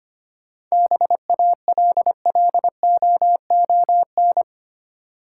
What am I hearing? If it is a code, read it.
Morse code, heard as a keyed tone, about 25 wpm, 700 Hz. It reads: BALLOON
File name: practice / 25wpm / Easy Words with Hints - 25wpm